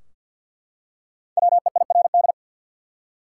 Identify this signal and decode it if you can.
Morse code, heard as a keyed tone, about 51 wpm, 700 Hz. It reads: WEIRD
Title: Five Letter Words - 50wpm